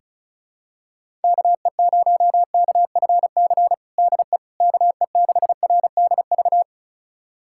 Morse code, heard as a keyed tone, about 34 wpm, 700 Hz. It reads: KE0KFC DE KE6RDV
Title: USA Callsign DE Callsign - 6 Characters - 35wpm